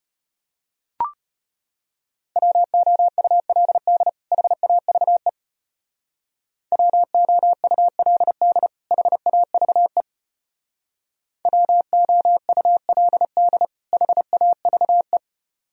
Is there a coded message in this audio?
Yes: Morse code from 2.4 to 15.2 s, heard as a keyed tone, about 33 wpm, 700 Hz. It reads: WOULD HAVE WOULD HAVE WOULD HAVE